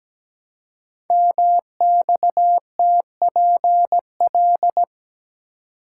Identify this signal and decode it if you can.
Morse code, heard as a keyed tone, about 17 wpm, 700 Hz. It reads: MXTPL
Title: Code Groups - 5 Letters - 17wpm